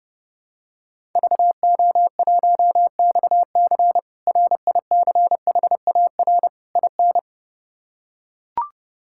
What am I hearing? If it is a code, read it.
Morse code, heard as a keyed tone, about 32 wpm, 700 Hz. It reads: VO1XC RICHAR IN
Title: K1USN SST QSO Practice - 30wpm